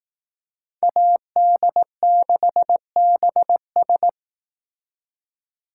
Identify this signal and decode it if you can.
Morse code, heard as a keyed tone, about 19 wpm, 700 Hz. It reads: AD6BS
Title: Speed Racing - USA Call Signs 15wpm